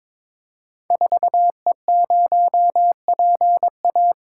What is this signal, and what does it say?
Morse code, heard as a keyed tone, about 22 wpm, 700 Hz. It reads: 4E0PA
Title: Code Groups - 5 Letter:Numbers - 22wpm